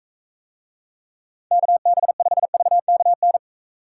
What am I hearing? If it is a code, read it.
Morse code, heard as a keyed tone, about 45 wpm, 700 Hz. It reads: KB5VKN